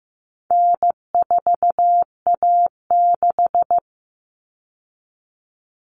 Morse code, heard as a keyed tone, about 15 wpm, 700 Hz. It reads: N4A6
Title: All Cut Numbers - 4-Digit 15wpm